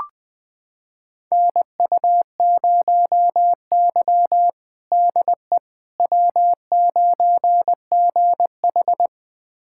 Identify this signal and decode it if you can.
Morse code, heard as a keyed tone, about 20 wpm, 700 Hz. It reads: NU0Y DE W9GH
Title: USA Callsign DE Callsign - 4 Characters - 20wpm